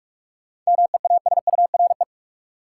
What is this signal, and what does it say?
Morse code, heard as a keyed tone, about 41 wpm, 700 Hz. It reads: MEASURE
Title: Sentences from Top 1000 Words - Review Part 1 - New Word Drill - 45wpm